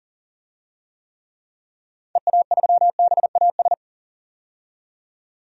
Morse code, heard as a keyed tone, about 40 wpm, 700 Hz. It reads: EA3BAS
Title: DXCC-100 Call Signs - 6 Characters - 40wpm